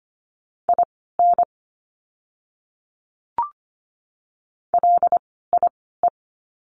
Morse code, heard as a keyed tone, about 25 wpm, spaced as for 16 wpm, 700 Hz. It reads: IN LIE
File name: course / 25:10wpm / 094 Lesson 16f - Letter c - Words 25wpm